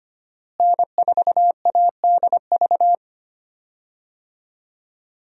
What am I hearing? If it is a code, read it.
Morse code, heard as a keyed tone, about 25 wpm, 700 Hz. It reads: N4ADV
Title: USA Call Signs - 5 Characters - Repeated 3x - 25wpm